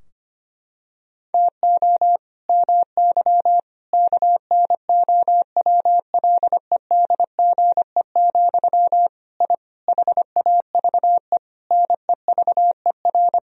Morse code, heard as a keyed tone, about 25 wpm, 700 Hz. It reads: TO MY KNOWLEDGE, I HAVE NEVER
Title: Sentences from Top 1000 Words - Review Part 3 - 25wpm